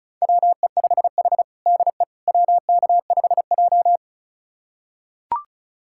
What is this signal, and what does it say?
Morse code, heard as a keyed tone, about 34 wpm, 700 Hz. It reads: WE5H DE WK5J
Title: USA Callsign DE Callsign - 4 Characters - 35wpm